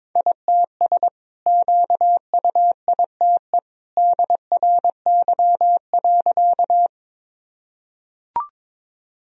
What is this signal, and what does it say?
Morse code, heard as a keyed tone, about 21 wpm, 700 Hz. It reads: ITS QUITE DRY.
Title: Sentences from Top 1000 Words - Review Part 3 - 22wpm